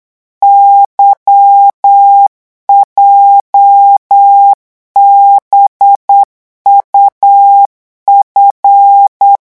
Morse code, heard as a keyed tone, about 8 wpm, 800 Hz. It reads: YJBUF
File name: morse mp3 / revision8-vitesse 8 mots